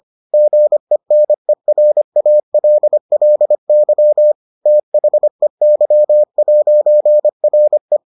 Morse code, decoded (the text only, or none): GENERALLY THEY'RE